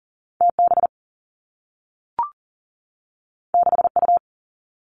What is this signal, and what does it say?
Morse code, heard as a keyed tone, about 39 wpm, 700 Hz. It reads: TB 6U
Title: Two Challenging Letter-Number Combinations 40wpm